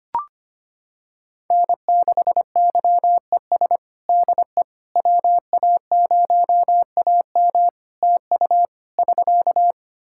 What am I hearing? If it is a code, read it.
Morse code, heard as a keyed tone, about 25 wpm, 700 Hz. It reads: N6YES DE WA0AM TU <SK>
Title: Practice QSO - Exchange 4 - Ending - 25wpm